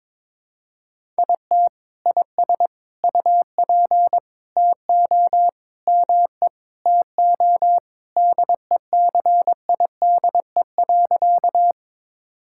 Morse code, heard as a keyed tone, about 21 wpm, 700 Hz. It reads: IT IS UP TO ME TO DECIDE.
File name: practice / 22wpm / Sentences with 2-Word Phrases - 22wpm